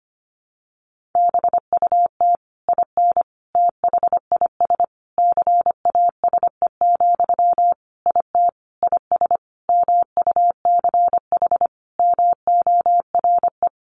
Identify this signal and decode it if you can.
Morse code, heard as a keyed tone, about 25 wpm, 700 Hz. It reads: BUT IN THIS CASE, IT IS MUCH MORE
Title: Sentences from Top 1200 Words 25wpm